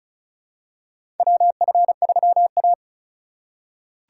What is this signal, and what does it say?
Morse code, heard as a keyed tone, about 36 wpm, 700 Hz. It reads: WF3A
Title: USA Call Signs - 4 Characters - 2x Character Spacing - 35wpm